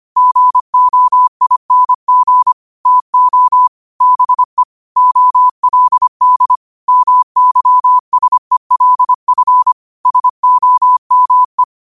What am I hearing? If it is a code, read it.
Morse code, heard as a keyed tone, about 25 wpm, 1000 Hz. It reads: GOING TO BE OLD MYSELF SOME
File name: qotdfiles / 25wpm